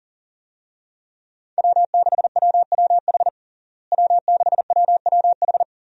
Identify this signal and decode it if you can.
Morse code, heard as a keyed tone, about 40 wpm, 700 Hz. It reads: W6WWH W6WWH
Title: USA Call Signs - 5 Characters - Repeated 3x - 40wpm